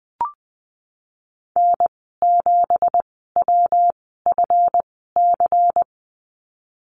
Morse code, heard as a keyed tone, about 20 wpm, spaced as for 15 wpm, 700 Hz. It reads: N7WFC
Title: 173 Lesson 28g - Number 9 - Call Signs 20wpm